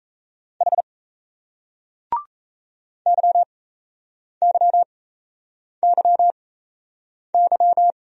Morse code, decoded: H YYYY

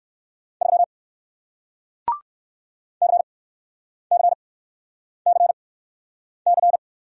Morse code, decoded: 3 CCCC

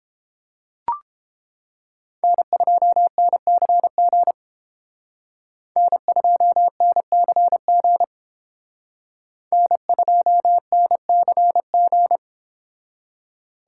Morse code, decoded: N2NCG N2NCG N2NCG

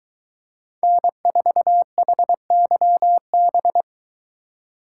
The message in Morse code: N4HYB

